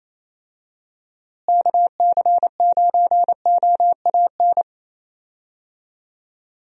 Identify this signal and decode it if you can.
Morse code, heard as a keyed tone, about 28 wpm, 700 Hz. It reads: KC9OAN
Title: USA Call Signs - 6 Characters - Repeated 3x - 28wpm